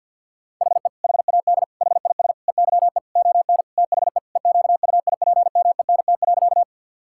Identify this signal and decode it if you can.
Morse code, heard as a keyed tone, about 52 wpm, 700 Hz. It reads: HE HAD HIS EYE ON THE EQUIPMENT.
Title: Sentences from Top 1000 Words - Review Part 5 - 50wpm